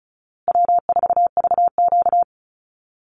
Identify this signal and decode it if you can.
Morse code, heard as a keyed tone, about 34 wpm, 700 Hz. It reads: W4VQ
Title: Easy Call Signs Repeated 3x - 35wpm